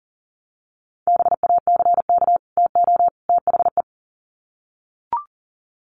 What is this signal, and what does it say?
Morse code, heard as a keyed tone, about 40 wpm, 700 Hz. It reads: BACK TO THE
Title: Sets of 3 Words 40wpm